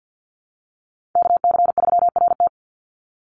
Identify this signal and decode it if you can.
Morse code, heard as a keyed tone, about 48 wpm, 700 Hz. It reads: KC3RT
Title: NA SOTA Chaser Callsigns 50wpm